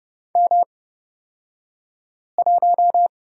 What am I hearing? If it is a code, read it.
Morse code, heard as a keyed tone, about 30 wpm, spaced as for 5 wpm, 700 Hz. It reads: M1
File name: Lesson 36d - BT - Two Characters 30wpm